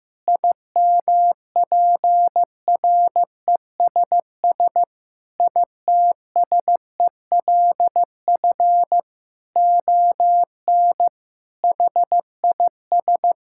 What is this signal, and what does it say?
Morse code, heard as a keyed tone, about 15 wpm, 700 Hz. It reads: IMPRESS ITSELF ON HIS
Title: Sentences from Top 1600 Words 15wpm